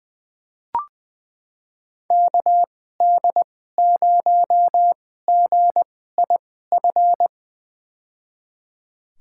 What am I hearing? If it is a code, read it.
Morse code, heard as a keyed tone, about 21 wpm, spaced as for 15 wpm, 700 Hz. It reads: KD0GIF
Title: USA Call Signs - 6 Characters - 2x Character Spacing - 20wpm